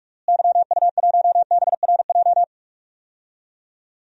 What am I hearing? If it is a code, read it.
Morse code, heard as a keyed tone, about 42 wpm, 700 Hz. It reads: YU1BRJ
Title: DXCC-100 Call Signs - 6 Characters - Repeated 2x - 45wpm